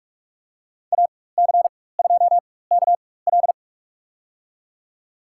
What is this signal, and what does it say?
Morse code, heard as a keyed tone, about 46 wpm, spaced as for 21 wpm, 700 Hz. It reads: AC2KL
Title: USA Call Signs - 5 Characters - 4x Character Spacing - 45wpm